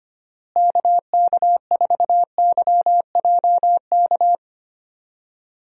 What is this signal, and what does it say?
Morse code, heard as a keyed tone, about 25 wpm, 700 Hz. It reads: KK4YJK
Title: USA Call Signs - 6 Characters - Repeated 2x - 25wpm